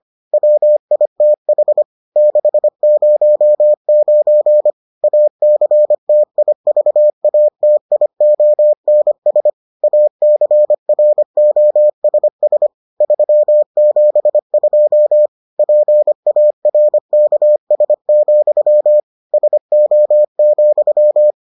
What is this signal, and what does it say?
Morse code, heard as a keyed tone, about 25 wpm, 600 Hz. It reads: WITH 609 ACTIVATIONS ACROSS 372 PARKS, SO,